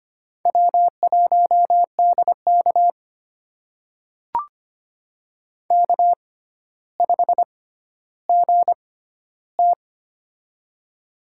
Morse code, heard as a keyed tone, about 25 wpm, spaced as for 13 wpm, 700 Hz. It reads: W1DK K5GT
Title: USA Call Signs - 6x Character Spacing - 25wpm